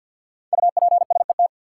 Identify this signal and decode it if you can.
Morse code, heard as a keyed tone, about 52 wpm, 700 Hz. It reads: UPSET
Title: Sentences from Top 800 Words - New Word Drill - 50wpm